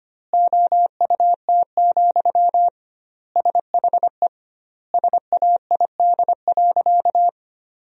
Morse code, heard as a keyed tone, about 25 wpm, 700 Hz. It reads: OUT, SHE SAID.
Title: 2x Word-Spacing - Sentences from Top 500 Words - 25wpm